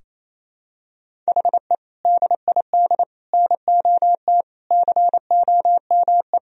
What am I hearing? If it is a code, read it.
Morse code, heard as a keyed tone, about 28 wpm, 700 Hz. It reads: HE DID NOT COME
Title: Sentences from Top 1000 Words - Review Part 3 - 28wpm